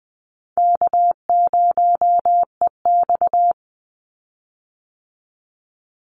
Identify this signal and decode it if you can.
Morse code, heard as a keyed tone, about 20 wpm, 700 Hz. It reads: K0EX
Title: Easy Call Signs Repeated 2x - 20wpm